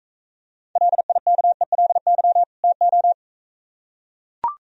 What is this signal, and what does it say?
Morse code, heard as a keyed tone, about 43 wpm, 700 Hz. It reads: LIKELY TO